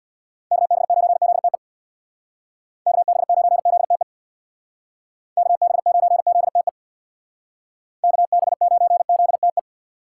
Morse code, heard as a keyed tone, about 61 wpm, 700 Hz. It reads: KB9ZTE KB9ZTE KB9ZTE KB9ZTE
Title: Speed Racing - USA Call Signs 50wpm